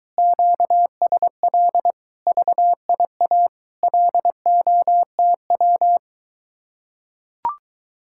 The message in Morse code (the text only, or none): QSL VIA LOTW